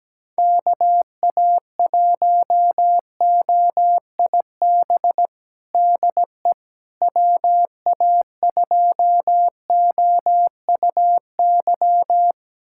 Morse code, heard as a keyed tone, about 17 wpm, 700 Hz. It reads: KA1OIB DE WA2OUY
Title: USA Callsign DE Callsign 17wpm